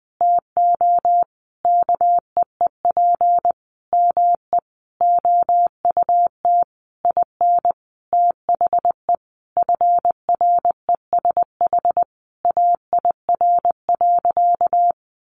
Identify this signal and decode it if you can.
Morse code, heard as a keyed tone, about 20 wpm, 700 Hz. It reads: TO KEEP ME OUT IN THE FRESH AIR.
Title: Sentences from Top 1200 Words 20wpm